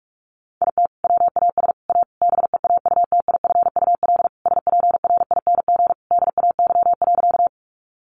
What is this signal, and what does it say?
Morse code, heard as a keyed tone, about 44 wpm, 700 Hz. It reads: IT WAS A BEAUTIFUL SPRING DAY.
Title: Sentences from Top 1200 Words 45wpm